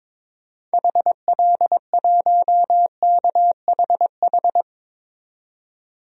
Morse code, heard as a keyed tone, about 22 wpm, 700 Hz. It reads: HL1KHH
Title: DXCC-100 Call Signs - 6 Characters - 22wpm